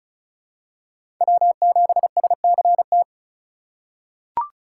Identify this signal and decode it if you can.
Morse code, heard as a keyed tone, about 36 wpm, 700 Hz. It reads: W7SCT
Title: USA Call Signs - 4x Character Spacing - 35wpm